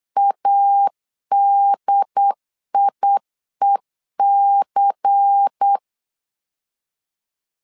Morse code, heard as a keyed tone, about 8 wpm, 800 Hz. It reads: ADIEC
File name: LEZIONI / 5 LBVC / 5 LBVC 8wpm 2935